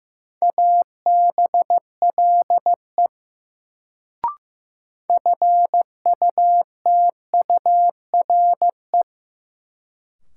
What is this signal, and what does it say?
Morse code, heard as a keyed tone, about 15 wpm, 700 Hz. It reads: ABLE FUTURE
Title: Sentences from Top 700 Words - New Word Drill - 15wpm